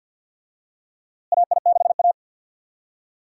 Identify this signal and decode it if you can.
Morse code, heard as a keyed tone, about 52 wpm, 700 Hz. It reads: AI6A
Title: USA Call Signs - 4 Characters - Repeated 2x - 50wpm